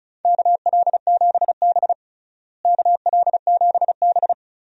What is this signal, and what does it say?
Morse code, heard as a keyed tone, about 34 wpm, 700 Hz. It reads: KL7B KL7B
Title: DXCC-100 Call Signs - 4 Characters - Repeated 2x - 35wpm